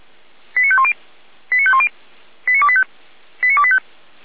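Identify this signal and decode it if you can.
Five-tone selective call (ZVEI), heard as quick runs of five tones, tones of about 70 ms. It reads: 89510 89510 89285 89285
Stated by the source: Five-tone ZVEI selective call received on an SDR.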